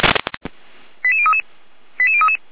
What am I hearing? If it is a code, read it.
Five-tone selective call (ZVEI), heard as quick runs of five tones, tones of about 70 ms. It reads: 88033 88033
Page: /455/SDR/ZVEI/89412